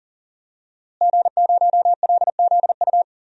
Morse code, heard as a keyed tone, about 40 wpm, 700 Hz. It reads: G0LZU
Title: DXCC-100 Call Signs - 5 Characters - Repeated 3x - 40wpm